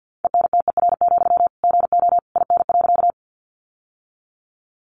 Morse code, heard as a keyed tone, about 50 wpm, 700 Hz. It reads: ENTER, GO IN.
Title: Sentences from Top 1300 Words 50wpm